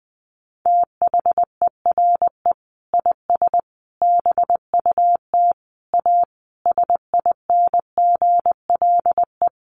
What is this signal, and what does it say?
Morse code, heard as a keyed tone, about 20 wpm, 700 Hz. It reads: THERE IS BUT A SINGLE
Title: Sentences from Top 1400 Words 20wpm